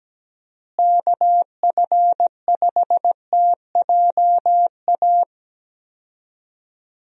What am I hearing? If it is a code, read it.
Morse code, heard as a keyed tone, about 17 wpm, 700 Hz. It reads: KF5TJA
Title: USA Call Signs - 6 Characters - Repeated 2x - 17wpm